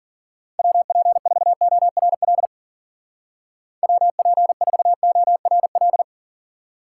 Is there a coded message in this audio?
Yes: Morse code from 0.6 to 6.0 s, heard as a keyed tone, about 52 wpm, 700 Hz. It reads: WP4ORL WP4ORL